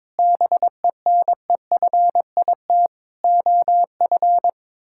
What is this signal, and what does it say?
Morse code, heard as a keyed tone, about 22 wpm, 700 Hz. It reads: BENEFIT OF
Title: Sentences from Top 1000 Words - Review Part 6 - 22wpm